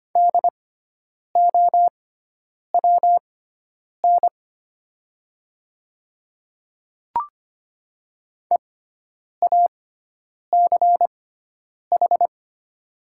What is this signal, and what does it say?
Morse code, heard as a keyed tone, about 25 wpm, spaced as for 9 wpm, 700 Hz. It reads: DOWN EACH
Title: 6x Character Spacing - Top 100 Words - 25wpm